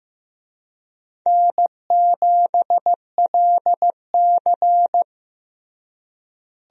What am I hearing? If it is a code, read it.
Morse code, heard as a keyed tone, about 15 wpm, 700 Hz. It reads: N7LC
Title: USA Call Signs - 4 Characters - Repeated 2x - 15wpm